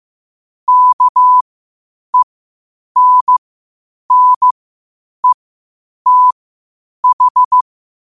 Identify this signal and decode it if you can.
Morse code, heard as a keyed tone, about 15 wpm, spaced as for 8 wpm, 1000 Hz. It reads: KENNETH
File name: qotdfiles / 8wpm